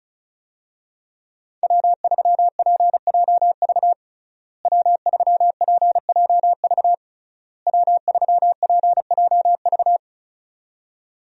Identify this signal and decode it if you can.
Morse code, heard as a keyed tone, about 34 wpm, 700 Hz. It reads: W3PJV W3PJV W3PJV
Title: USA Call Signs - 5 Characters - Repeated 3x - 35wpm